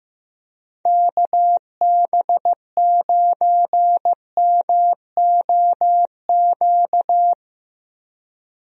Morse code, heard as a keyed tone, about 15 wpm, 700 Hz. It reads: KB9MOQ
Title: USA Call Signs - 6 Characters - 15wpm